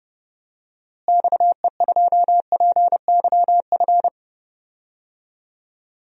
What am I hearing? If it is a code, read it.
Morse code, heard as a keyed tone, about 30 wpm, 700 Hz. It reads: XE2PYF
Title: DXCC-100 Call Signs - 6 Characters - Repeated 2x - 30wpm